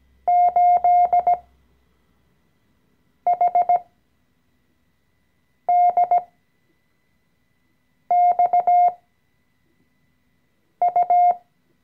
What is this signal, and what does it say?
Morse code, heard as a keyed tone, about 17 wpm, spaced as for 4 wpm, 710 Hz. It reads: 8HDXU
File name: CW-Online-Kurs / Anfänger / Lektion 014